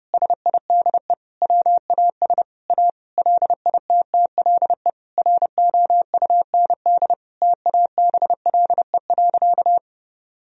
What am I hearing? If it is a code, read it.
Morse code, heard as a keyed tone, about 32 wpm, 700 Hz. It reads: SIDE WAS A LITTLE ROUND TABLE.
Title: Sentences from Top 1000 Words - Review Part 2 - 30wpm